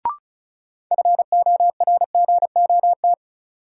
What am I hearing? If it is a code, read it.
Morse code, heard as a keyed tone, about 36 wpm, 700 Hz. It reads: FORGOT
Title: Sentences from Top 1400 Words - New Word Drill - 35wpm